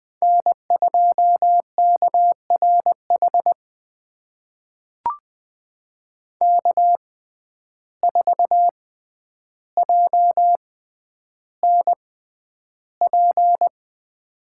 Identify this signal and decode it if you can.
Morse code, heard as a keyed tone, about 20 wpm, spaced as for 10 wpm, 700 Hz. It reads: N2KRH K4JNP
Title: USA Call Signs - 5 Characters - 6x Character Spacing - 20wpm